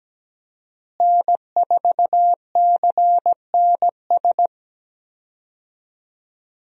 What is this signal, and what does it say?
Morse code, heard as a keyed tone, about 17 wpm, 700 Hz. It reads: N4CNS
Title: USA Call Signs - 5 Characters - Repeated 2x - 17wpm